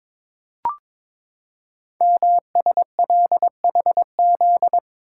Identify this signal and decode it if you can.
Morse code, heard as a keyed tone, about 22 wpm, 700 Hz. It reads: MSLHZ
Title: Code Groups - 5 Letter:Numbers - 22wpm